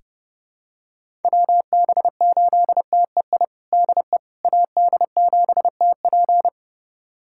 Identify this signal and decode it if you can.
Morse code, heard as a keyed tone, about 30 wpm, 700 Hz. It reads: WB8TEI DE AD7TP
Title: USA Callsign DE Callsign 30wpm